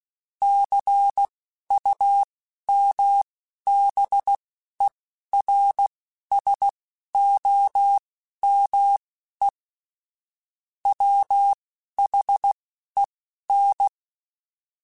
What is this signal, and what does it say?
Morse code, heard as a keyed tone, about 16 wpm, spaced as for 11 wpm, 800 Hz. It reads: CUMBERSOME WHEN
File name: Technical Subjects / Technical Subjects-mp3s / AND Gate @10